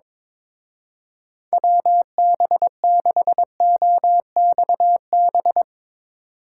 Morse code, heard as a keyed tone, about 24 wpm, 700 Hz. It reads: WB6OXB
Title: Speed Racing - USA 2x3 Call Signs 22wpm